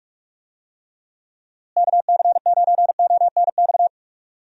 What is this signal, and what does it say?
Morse code, heard as a keyed tone, about 41 wpm, 700 Hz. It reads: KC9ONX
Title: USA Call Signs - 6 Characters - Repeated 3x - 45wpm